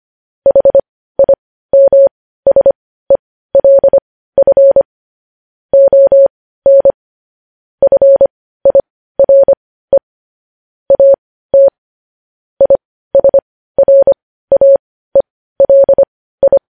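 Morse code, heard as a keyed tone, about 25 wpm, spaced as for 15 wpm, 550 Hz. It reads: HIMSELF ON FIRE AT ISRAELI
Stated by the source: News headlines sent in Morse code at 15 wpm.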